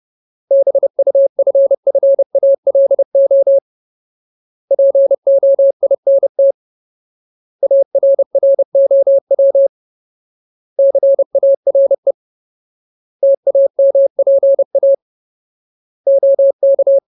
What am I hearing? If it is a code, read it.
Morse code, heard as a keyed tone, about 30 wpm, 550 Hz. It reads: BUFFALO POINT ARROW CARE TAMPA OK
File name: Ultimate Word List / 30WPM Ultimate